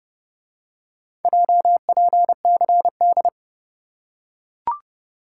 Morse code, heard as a keyed tone, about 29 wpm, 700 Hz. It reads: JPCD